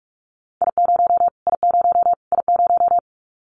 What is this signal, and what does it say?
Morse code, heard as a keyed tone, about 44 wpm, 700 Hz. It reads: I0 I0 I0
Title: Callsign Prefixes for Top 25 DXCC - Repeated 3x - 45wpm